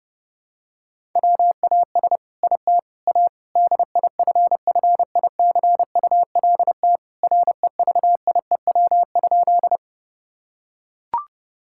Morse code, heard as a keyed tone, about 30 wpm, 700 Hz. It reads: WAS IT A DIFFICULT REVIEW?
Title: Sentences from Top 1000 Words - Review Part 6 - 30wpm